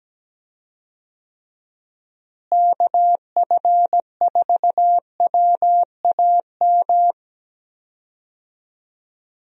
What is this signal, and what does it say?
Morse code, heard as a keyed tone, about 17 wpm, 700 Hz. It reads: KF4WAM
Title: USA Call Signs - 6 Characters - Repeated 2x - 17wpm